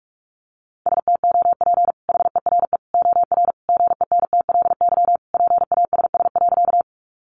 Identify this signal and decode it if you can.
Morse code, heard as a keyed tone, about 44 wpm, 700 Hz. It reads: STOP HERE OR GENTLY PASS.
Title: Sentences from Top 1500 Words 45wpm